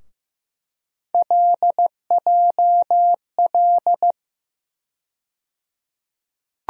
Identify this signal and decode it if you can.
Morse code, heard as a keyed tone, about 15 wpm, 700 Hz. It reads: LJL
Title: USA Call Sign Postfixes - 3 Letters 15wpm